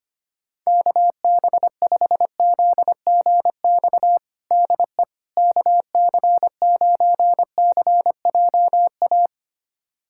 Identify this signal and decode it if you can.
Morse code, heard as a keyed tone, about 25 wpm, 700 Hz. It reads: KB5ZGX DE KC9CJA
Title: USA Callsign DE Callsign - 6 Characters - 25wpm